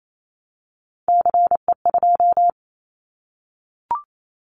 Morse code, heard as a keyed tone, about 28 wpm, 700 Hz. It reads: CE2
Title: Callsign Prefixes for Top 100 DXCC - 28wpm